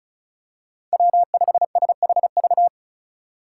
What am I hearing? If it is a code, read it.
Morse code, heard as a keyed tone, about 36 wpm, 700 Hz. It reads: W5SHV
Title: USA Call Signs - 2x Character Spacing - 35wpm